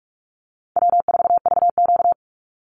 Morse code, heard as a keyed tone, about 44 wpm, 700 Hz. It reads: W4VQ
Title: Easy Call Signs Repeated 2x - 45wpm